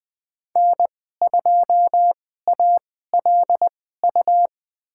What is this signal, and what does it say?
Morse code, heard as a keyed tone, about 21 wpm, spaced as for 15 wpm, 700 Hz. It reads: N2ALU
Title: USA Call Signs - 2x Character Spacing - 20wpm